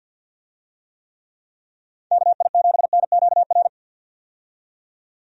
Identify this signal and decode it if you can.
Morse code, heard as a keyed tone, about 53 wpm, 700 Hz. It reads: KI7NQR
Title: USA Call Signs - 6 Characters - 50wpm